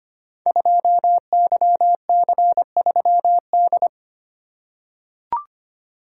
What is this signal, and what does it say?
Morse code, heard as a keyed tone, about 25 wpm, 700 Hz. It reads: 2YC3D
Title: Code Groups - 5 Letter:Numbers - 25wpm